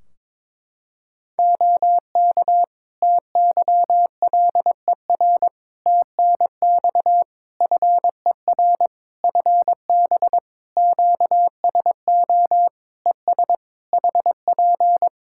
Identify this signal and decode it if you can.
Morse code, heard as a keyed tone, about 22 wpm, 700 Hz. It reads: OK TYLER TNX FER FB QSO ES HP
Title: Practice QSO - Exchange 4 - Ending - 22wpm